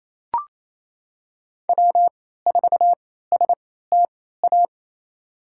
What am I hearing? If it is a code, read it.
Morse code, heard as a keyed tone, about 28 wpm, spaced as for 16 wpm, 700 Hz. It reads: W4STA
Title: USA Call Signs - 3x Character Spacing - 28wpm